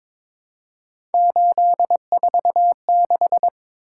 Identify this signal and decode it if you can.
Morse code, heard as a keyed tone, about 22 wpm, 700 Hz. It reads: 846
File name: practice / 22wpm / Three Letter-Number Combinations - 22wpm